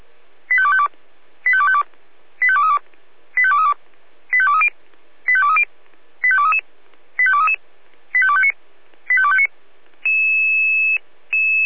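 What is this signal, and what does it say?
Five-tone selective call (ZVEI), heard as quick runs of five tones, tones of about 70 ms. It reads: 85252 85252 85232 85232 85230 85230 85233 85233 85259 85259
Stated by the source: ZVEI five-tone selective call received on an SDR.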